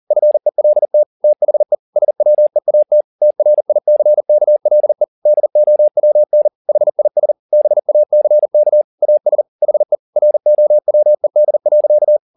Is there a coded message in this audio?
Yes: Morse code from 0.1 to 12.2 s, heard as a keyed tone, about 40 wpm, 600 Hz. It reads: FELT THE SWEAT TRICKLE DOWN HIS BACK AS HE ROWED.